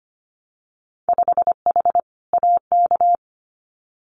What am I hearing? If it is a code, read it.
Morse code, heard as a keyed tone, about 25 wpm, 690 Hz. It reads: 5H AK
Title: Winter Field Day Exchanges Encore 25wpm